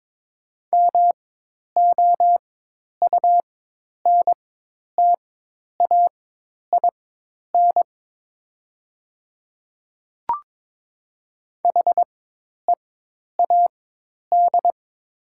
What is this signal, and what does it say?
Morse code, heard as a keyed tone, about 22 wpm, spaced as for 10 wpm, 700 Hz. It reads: MOUNTAIN HEAD
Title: 4x Character Spacing - Top 300 Words - 22wpm